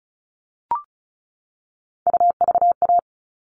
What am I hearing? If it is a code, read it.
Morse code, heard as a keyed tone, about 34 wpm, 700 Hz. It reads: UVA
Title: All Cut Numbers - 3-Digit 35wpm